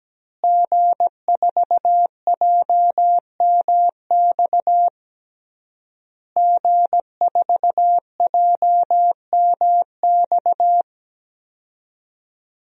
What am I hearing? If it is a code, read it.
Morse code, heard as a keyed tone, about 17 wpm, 700 Hz. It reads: G4JMX G4JMX
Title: DXCC-100 Call Signs - 5 Characters - Repeated 3x - 17wpm